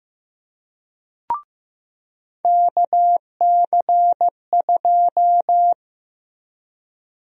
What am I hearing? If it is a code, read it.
Morse code, heard as a keyed tone, about 15 wpm, 700 Hz. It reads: KC2
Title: USA Call Sign Prefixes - Repeated 2x - 15wpm